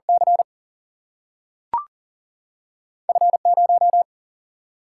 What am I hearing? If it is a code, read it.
Morse code, heard as a keyed tone, about 39 wpm, 700 Hz. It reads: C F 0
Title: Single-Letter Single-Number 40wpm